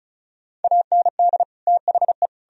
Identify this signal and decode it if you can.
Morse code, heard as a keyed tone, about 34 wpm, 700 Hz. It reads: AND THE
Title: Sentences from Top 1000 Words - Review Part 1 - 35wpm